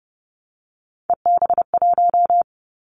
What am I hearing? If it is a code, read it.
Morse code, heard as a keyed tone, about 30 wpm, 700 Hz. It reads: EB1
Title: Callsign Prefixes for Top 25 DXCC - Repeated 3x - 30wpm